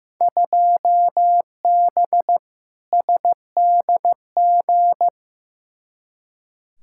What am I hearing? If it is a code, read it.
Morse code, heard as a keyed tone, about 15 wpm, 700 Hz. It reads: 2B SDG
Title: Field Day Exchanges Encore - 15wpm